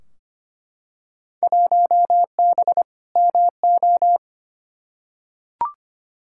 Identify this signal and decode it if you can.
Morse code, heard as a keyed tone, about 25 wpm, 700 Hz. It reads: 1B MO
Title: Field Day Exchanges - 25wpm